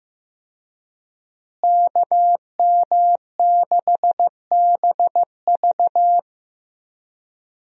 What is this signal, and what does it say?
Morse code, heard as a keyed tone, about 16 wpm, 700 Hz. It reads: KM6BV
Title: USA Call Signs - 5 Characters - Repeated 2x - 15wpm